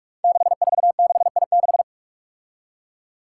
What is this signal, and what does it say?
Morse code, heard as a keyed tone, about 44 wpm, 700 Hz. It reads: 646I6
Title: Five Challenging Letter-Number Combinations 45wpm